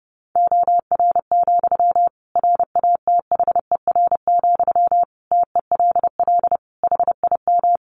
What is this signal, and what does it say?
Morse code, heard as a keyed tone, about 30 wpm, 700 Hz. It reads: OR, RATHER, TELL HIM
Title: Sentences from Top 700 Words 30wpm